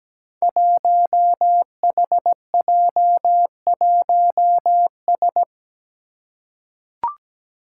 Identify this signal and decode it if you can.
Morse code, heard as a keyed tone, about 17 wpm, 700 Hz. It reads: 1HJ1S
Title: Five Challenging Letter-Number Combinations 17wpm